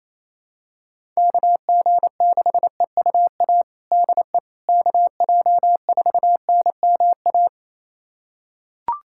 Morse code, heard as a keyed tone, about 28 wpm, 700 Hz. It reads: KG6EUA DE KJ4NMA
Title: USA Callsign DE Callsign - 6 Characters - 28wpm